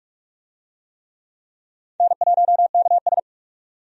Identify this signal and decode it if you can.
Morse code, heard as a keyed tone, about 43 wpm, 700 Hz. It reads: N1KS
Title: USA Call Signs - 4 Characters - 45wpm